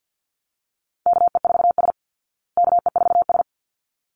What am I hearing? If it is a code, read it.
Morse code, heard as a keyed tone, about 48 wpm, 700 Hz. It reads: KE4S KE4S
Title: Call Signs Repeated 3x - 50wpm